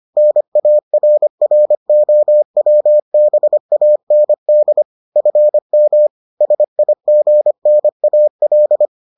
Morse code, heard as a keyed tone, about 25 wpm, 600 Hz. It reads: NARROWBAND FM SIGNAL